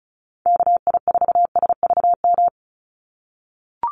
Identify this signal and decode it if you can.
Morse code, heard as a keyed tone, about 34 wpm, 700 Hz. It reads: KI4SVM
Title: NA SOTA Chaser Callsigns 35wpm